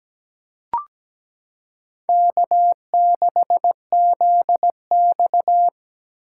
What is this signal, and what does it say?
Morse code, heard as a keyed tone, about 17 wpm, 700 Hz. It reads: K6ZX
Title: K1USN SST QSO Practice - 17wpm